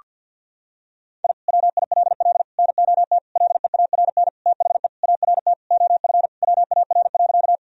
Morse code, heard as a keyed tone, about 50 wpm, 700 Hz. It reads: I WILL NOT LEARN THE ART OF WAR.
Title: Sentences from Top 1000 Words - Review Part 3 - 50wpm